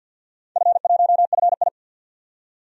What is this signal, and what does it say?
Morse code, heard as a keyed tone, about 51 wpm, 700 Hz. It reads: F1FI
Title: DXCC-100 Call Signs - 4 Characters - 50wpm